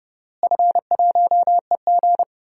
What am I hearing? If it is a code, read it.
Morse code, heard as a keyed tone, about 30 wpm, 700 Hz. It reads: F1EG
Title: DXCC-100 Call Signs - 4 Characters - Repeated 2x - 30wpm